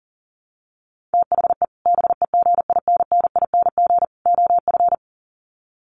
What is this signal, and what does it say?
Morse code, heard as a keyed tone, about 40 wpm, 700 Hz. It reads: THE BEGINNING OF
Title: Sets of 3 Words 40wpm